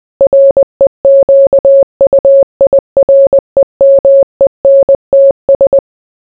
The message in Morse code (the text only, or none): REQUIREMENTS